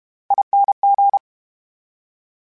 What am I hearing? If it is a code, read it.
Morse code, heard as a keyed tone, about 32 wpm, 800 Hz. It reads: ING
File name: Special INGs / Special INGs-mp3s / VerySlowToFaster Ing @30